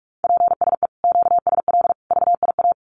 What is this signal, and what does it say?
Morse code, heard as a keyed tone, about 44 wpm, 700 Hz. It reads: PSE QSL VIA
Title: Common QSO Phrases 45wpm